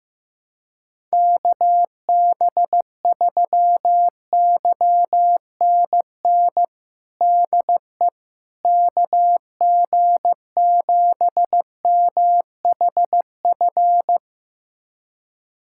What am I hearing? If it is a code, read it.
Morse code, heard as a keyed tone, about 15 wpm, 700 Hz. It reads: KB3YNN DE KG7MHF
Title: USA Callsign DE Callsign - 6 Characters - 15wpm